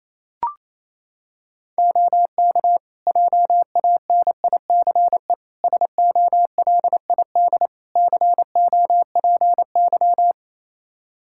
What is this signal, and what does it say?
Morse code, heard as a keyed tone, about 28 wpm, 700 Hz. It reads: OK JANICE SOLID COPY
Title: Practice QSO - Exchange 3 - More Info - 28wpm